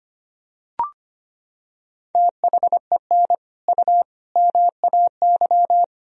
Morse code, heard as a keyed tone, about 25 wpm, 700 Hz. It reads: THEN U MAY
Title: Sentences from Top 100 Words and Abbreviations 25wpm